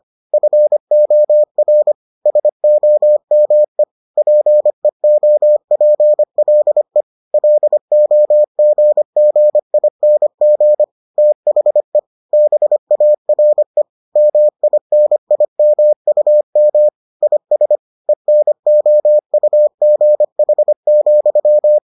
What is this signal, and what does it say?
Morse code, heard as a keyed tone, about 25 wpm, 600 Hz. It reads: FOR SOME PEOPLE LOGGING THE BARE MINIMUM IS ENOUGH,